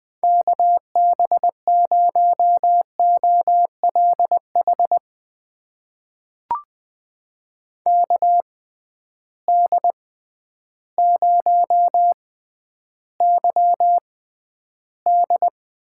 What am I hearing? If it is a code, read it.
Morse code, heard as a keyed tone, about 20 wpm, spaced as for 11 wpm, 700 Hz. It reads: KB0OLH KD0YD
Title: USA Call Signs - 6 Characters - 6x Character Spacing - 20wpm